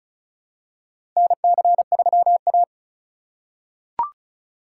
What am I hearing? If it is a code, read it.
Morse code, heard as a keyed tone, about 36 wpm, 700 Hz. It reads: NC3A
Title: USA Call Signs - 4 Characters - 4x Character Spacing - 35wpm